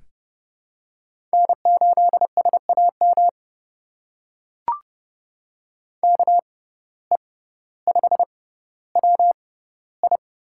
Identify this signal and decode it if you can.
Morse code, heard as a keyed tone, about 30 wpm, spaced as for 15 wpm, 700 Hz. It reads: N8SAM KE5WI
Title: USA Call Signs - 5 Characters - 6x Character Spacing - 30wpm